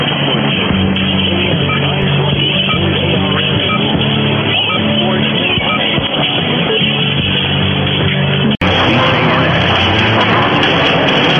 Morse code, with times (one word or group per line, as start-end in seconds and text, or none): none